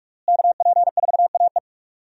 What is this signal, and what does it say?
Morse code, heard as a keyed tone, about 41 wpm, 700 Hz. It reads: KP4AE
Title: USA Call Signs - 5 Characters - Repeated 2x - 45wpm